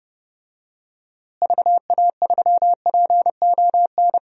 Morse code, heard as a keyed tone, about 30 wpm, 700 Hz. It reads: VA3PON
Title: DXCC-100 Call Signs - 6 Characters - Repeated 3x - 30wpm